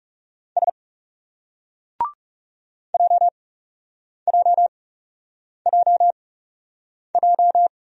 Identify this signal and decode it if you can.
Morse code, heard as a keyed tone, about 37 wpm, spaced as for 8 wpm, 700 Hz. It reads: S JJJJ
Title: Speed-Racing Single Letter-Number 30wpm